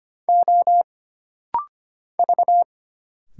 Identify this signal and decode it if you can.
Morse code, heard as a keyed tone, about 25 wpm, 700 Hz. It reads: O V
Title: Single Letters - Rapid-Fire 25wpm